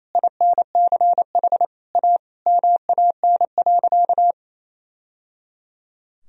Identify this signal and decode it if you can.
Morse code, heard as a keyed tone, about 28 wpm, 700 Hz. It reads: INCH A MAN.
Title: Sentences from Top 1000 Words - Review Part 3 - 28wpm